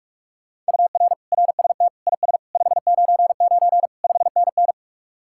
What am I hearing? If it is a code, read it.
Morse code, heard as a keyed tone, about 53 wpm, 700 Hz. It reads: UR RST IS 599 5NN
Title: Speed-Racing Common QSO Phrases - 45wpm